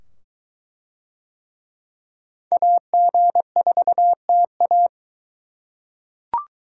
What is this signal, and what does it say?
Morse code, heard as a keyed tone, about 22 wpm, 700 Hz. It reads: AG4TA